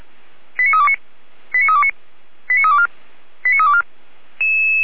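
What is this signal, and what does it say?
Five-tone selective call (ZVEI), heard as quick runs of five tones, tones of about 80 ms. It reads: 89329 89329 89324 89324
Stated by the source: ZVEI five-tone selective call received on an SDR.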